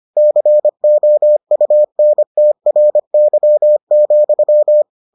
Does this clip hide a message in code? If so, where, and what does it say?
Yes: Morse code from 0.2 to 4.8 s, heard as a keyed tone, about 25 wpm, 600 Hz. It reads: COUNTRY,